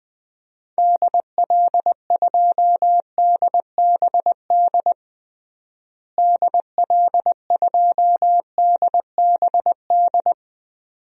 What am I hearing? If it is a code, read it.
Morse code, heard as a keyed tone, about 20 wpm, 700 Hz. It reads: DL2DBD DL2DBD
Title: DXCC-100 Call Signs - 6 Characters - Repeated 2x - 20wpm